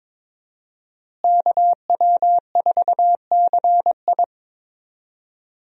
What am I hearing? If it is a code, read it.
Morse code, heard as a keyed tone, about 22 wpm, 700 Hz. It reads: KW4CI
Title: USA Call Signs - 5 Characters - Repeated 2x - 22wpm